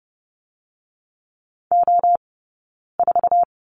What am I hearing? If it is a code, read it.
Morse code, heard as a keyed tone, about 30 wpm, 700 Hz. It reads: O 4
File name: Lesson 17d - Letter u - Two Characters 30wpm